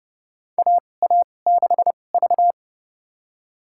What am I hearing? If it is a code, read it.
Morse code, heard as a keyed tone, about 30 wpm, spaced as for 22 wpm, 700 Hz. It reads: AA6V
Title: USA Call Signs - 4 Characters - 2x Character Spacing - 30wpm